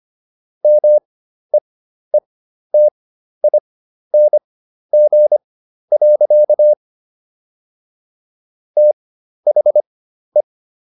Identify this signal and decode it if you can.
Morse code, heard as a keyed tone, about 25 wpm, spaced as for 12 wpm, 600 Hz. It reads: MEETING. THE